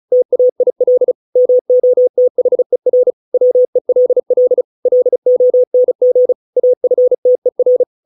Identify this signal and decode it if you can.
Morse code, heard as a keyed tone, about 34 wpm, 500 Hz. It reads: TAIL MOTHER WELL LONG AFTER